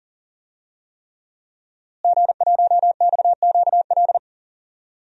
Morse code, heard as a keyed tone, about 40 wpm, 700 Hz. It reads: G1XQL
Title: DXCC-100 Call Signs - 5 Characters - Repeated 3x - 40wpm